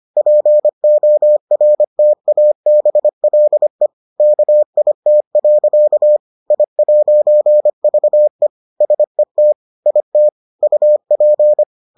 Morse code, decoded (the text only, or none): PORTABLE KIT. I'VE SET IT UP